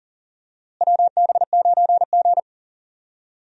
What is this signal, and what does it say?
Morse code, heard as a keyed tone, about 37 wpm, 700 Hz. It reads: WB9G
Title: USA Call Signs - 4 Characters - Repeated 2x - 40wpm